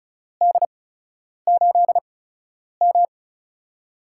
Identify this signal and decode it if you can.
Morse code, heard as a keyed tone, about 34 wpm, spaced as for 10 wpm, 700 Hz. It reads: D8M
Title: USA Call Signs - 4 Characters - 8x Character Spacing - 35wpm